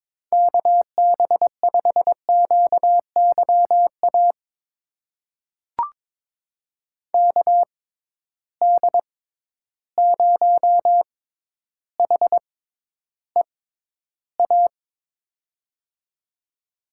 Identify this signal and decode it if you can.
Morse code, heard as a keyed tone, about 22 wpm, spaced as for 11 wpm, 700 Hz. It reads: KB5QYA KD0HEA
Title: USA Call Signs - 6 Characters - 6x Character Spacing - 22wpm